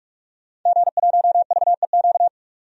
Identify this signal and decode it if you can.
Morse code, heard as a keyed tone, about 46 wpm, 700 Hz. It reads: G1VEQ
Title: DXCC-100 Call Signs - 5 Characters - 45wpm